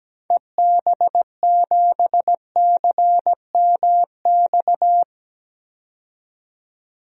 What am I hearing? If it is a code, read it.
Morse code, heard as a keyed tone, about 17 wpm, 700 Hz. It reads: EB7CMX
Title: DXCC-100 Call Signs - 6 Characters - 17wpm